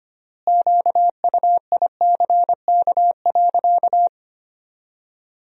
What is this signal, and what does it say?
Morse code, heard as a keyed tone, about 25 wpm, 700 Hz. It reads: QUICK.
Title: 4x Word-Spacing - Sentences from Top 500 Words - 25wpm